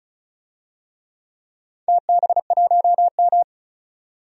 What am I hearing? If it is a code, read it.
Morse code, heard as a keyed tone, about 34 wpm, 700 Hz. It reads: TB1M
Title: DXCC-100 Call Signs - 4 Characters - 35wpm